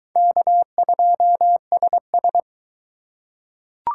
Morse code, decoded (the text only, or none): K2SS